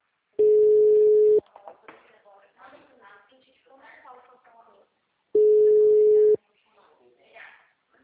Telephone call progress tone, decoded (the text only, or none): ringback tone